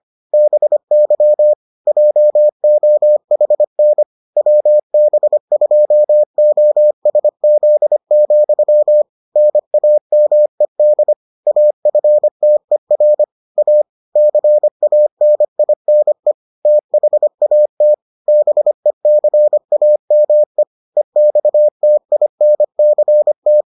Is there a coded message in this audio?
Yes: Morse code from 0.3 to 23.6 s, heard as a keyed tone, about 25 wpm, 600 Hz. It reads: BY JOHN WB2OSZ, NAMED AFTER A CANINE THAT BECAME EXTINCT